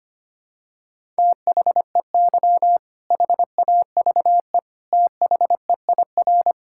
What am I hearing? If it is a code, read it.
Morse code, heard as a keyed tone, about 25 wpm, 700 Hz. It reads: THEY HAVE THEIR
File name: practice / 25wpm / All-Age Appropriate Jokes - 25wpm